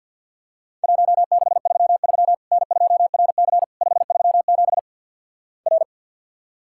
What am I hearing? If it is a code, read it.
Morse code, heard as a keyed tone, about 50 wpm, 700 Hz. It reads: 1633 N2RC 537 R
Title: Worked All Europe DX Contest - QTC Exchange with Repeats - 50wpm